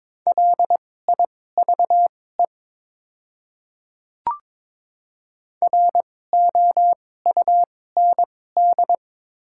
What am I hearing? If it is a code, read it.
Morse code, heard as a keyed tone, about 23 wpm, spaced as for 16 wpm, 700 Hz. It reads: LIVE ROUND
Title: 2x Character Spacing - Top 300 Words - 22wpm